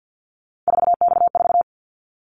Morse code, heard as a keyed tone, about 50 wpm, 700 Hz. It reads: 4X4
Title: Callsign Prefixes for Top 50 DXCC - 50wpm